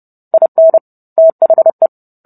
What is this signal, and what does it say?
Morse code, heard as a keyed tone, about 30 wpm, 650 Hz. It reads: IN THE